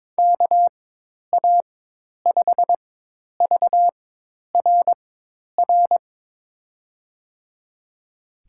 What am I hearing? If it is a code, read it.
Morse code, heard as a keyed tone, about 22 wpm, spaced as for 10 wpm, 700 Hz. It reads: KA5VRR
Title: USA Call Signs - 6 Characters - 4x Character Spacing - 22wpm